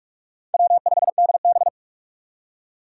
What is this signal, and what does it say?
Morse code, heard as a keyed tone, about 44 wpm, 700 Hz. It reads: W5DB
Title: USA Call Signs - 4 Characters - 4x Character Spacing - 45wpm